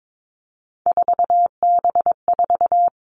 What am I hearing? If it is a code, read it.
Morse code, heard as a keyed tone, about 22 wpm, 700 Hz. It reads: 4B4
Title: Three Challenging Letter-Number Combinations 22wpm